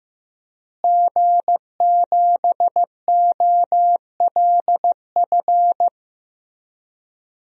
Morse code, heard as a keyed tone, about 15 wpm, 700 Hz. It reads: G7OLF
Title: DXCC-100 Call Signs - 5 Characters - Repeated 2x - 15wpm